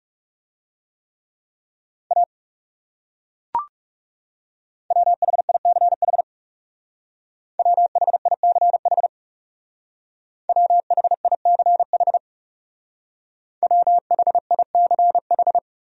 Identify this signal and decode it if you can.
Morse code, heard as a keyed tone, about 37 wpm, 700 Hz. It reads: A WHICH WHICH WHICH WHICH
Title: Speed-Racing Top 100 Words 30wpm